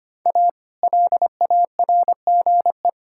Morse code, heard as a keyed tone, about 25 wpm, 700 Hz. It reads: A LARGE
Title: Pangram Challenge v2 - 25wpm